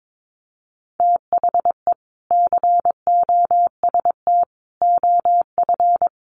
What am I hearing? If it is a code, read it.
Morse code, heard as a keyed tone, about 22 wpm, 700 Hz. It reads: THE COST OF
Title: Sets of 3 Words 22wpm